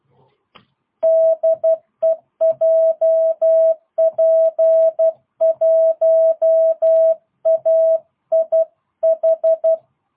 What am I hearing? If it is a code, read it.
Morse code, heard as a keyed tone, about 12 wpm, 660 Hz. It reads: DE JP1AIH
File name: jh8cbh / 2022cw